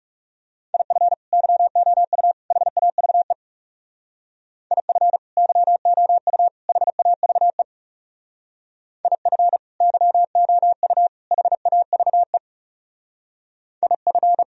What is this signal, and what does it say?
Morse code, heard as a keyed tone, about 36 wpm, 700 Hz. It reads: IF YOU HAVE IF YOU HAVE IF YOU HAVE IF
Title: Speed-Racing - Sets of 3 Words 30wpm